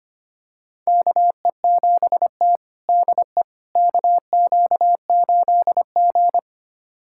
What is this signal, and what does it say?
Morse code, heard as a keyed tone, about 25 wpm, 700 Hz. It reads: KE7T DE KQ8G
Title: USA Callsign DE Callsign - 4 Characters - 25wpm